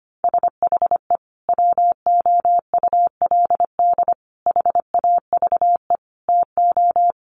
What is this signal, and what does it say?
Morse code, heard as a keyed tone, about 25 wpm, 700 Hz. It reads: SHE WOULD HAVE TO
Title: Sentences from Top 1200 Words 25wpm